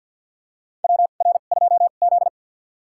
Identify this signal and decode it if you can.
Morse code, heard as a keyed tone, about 50 wpm, spaced as for 36 wpm, 700 Hz. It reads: WR2Z
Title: USA Call Signs - 4 Characters - 2x Character Spacing - 50wpm